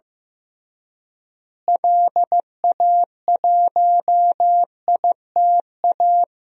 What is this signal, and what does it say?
Morse code, heard as a keyed tone, about 16 wpm, 700 Hz. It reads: LA1ITA
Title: DXCC-100 Call Signs - 6 Characters - Repeated 3x - 15wpm